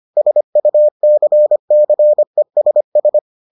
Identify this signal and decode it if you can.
Morse code, heard as a keyed tone, about 25 wpm, 600 Hz. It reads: SUCCESS